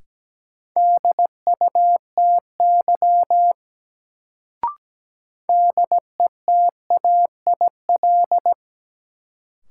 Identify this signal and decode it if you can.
Morse code, heard as a keyed tone, about 17 wpm, 700 Hz. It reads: DUTY DETAIL
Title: Sentences from Top 1500 Words - New Word Drill - 17wpm